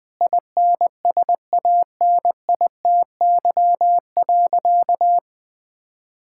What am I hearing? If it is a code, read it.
Morse code, heard as a keyed tone, about 20 wpm, 700 Hz. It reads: INSANITY.
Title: Sentences with 8-Word Phrases - 20wpm